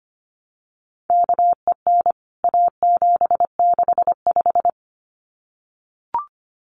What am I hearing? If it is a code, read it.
Morse code, heard as a keyed tone, about 25 wpm, 700 Hz. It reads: KEN A765
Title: CWT Exchange 25wpm